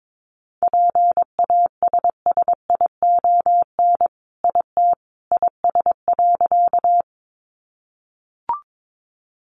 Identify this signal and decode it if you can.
Morse code, heard as a keyed tone, about 22 wpm, 700 Hz. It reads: PASSION IT IS.
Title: Sentences from Top 1300 Words 22wpm